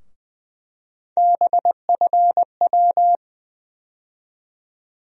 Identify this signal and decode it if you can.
Morse code, heard as a keyed tone, about 20 wpm, 700 Hz. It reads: BFW
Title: USA Call Sign Postfixes - 3 Letters 20wpm